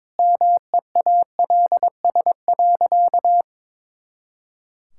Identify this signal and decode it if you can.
Morse code, heard as a keyed tone, about 22 wpm, 700 Hz. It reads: MEALS.
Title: Pangram Challenge v2 - 22wpm